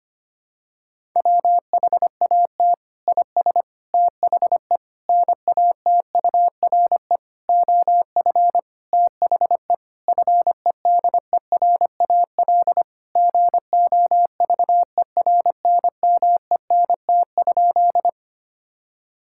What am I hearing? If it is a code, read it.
Morse code, heard as a keyed tone, about 25 wpm, 700 Hz. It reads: WHAT IS THE NATURE OF THE FEDERAL GOVERNMENT?
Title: Sentences from Top 1000 Words - Review Part 5 - 25wpm